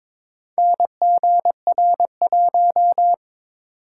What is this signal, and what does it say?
Morse code, heard as a keyed tone, about 22 wpm, 700 Hz. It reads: NGR1
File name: practice / 22wpm / Four Letter-Number Combinations - 22wpm